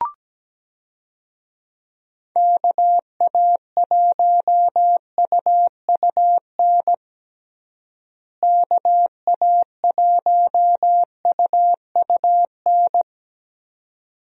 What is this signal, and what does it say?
Morse code, heard as a keyed tone, about 17 wpm, 700 Hz. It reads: KA1UUN KA1UUN
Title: USA Call Signs - 6 Characters - Repeated 2x - 17wpm